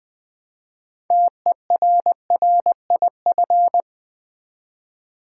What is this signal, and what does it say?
Morse code, heard as a keyed tone, about 20 wpm, 700 Hz. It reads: TERRIF
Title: Build A Word - 2 Letters At A Time - 8-Letter Words - 20wpm